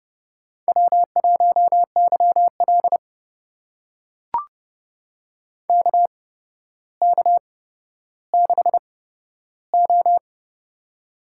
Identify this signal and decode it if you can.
Morse code, heard as a keyed tone, about 30 wpm, spaced as for 13 wpm, 700 Hz. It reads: W1YL KK6O
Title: USA Call Signs - 4 Characters - 8x Character Spacing - 30wpm